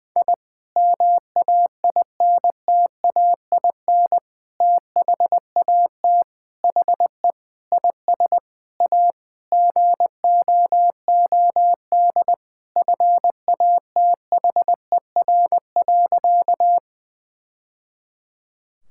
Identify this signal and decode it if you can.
Morse code, heard as a keyed tone, about 20 wpm, 700 Hz. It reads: I MAINTAIN THAT HE IS A GOOD FATHER.
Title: Sentences from Top 1000 Words - Review Part 5 - 20wpm